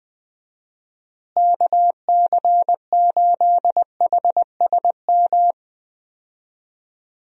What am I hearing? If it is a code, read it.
Morse code, heard as a keyed tone, about 20 wpm, 700 Hz. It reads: KC8HSM
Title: USA Call Signs - 6 Characters - Repeated 2x - 20wpm